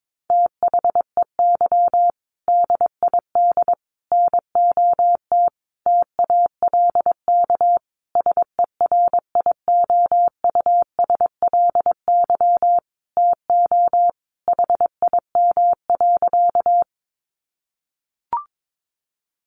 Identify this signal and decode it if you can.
Morse code, heard as a keyed tone, about 22 wpm, 700 Hz. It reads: THEY DID NOT TALK SERIOUSLY TO HIM.
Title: Sentences from Top 1200 Words 22wpm